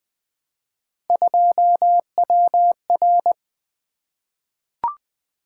Morse code, heard as a keyed tone, about 20 wpm, 700 Hz. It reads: 2WR
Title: Three Letter-Number Combinations - 20wpm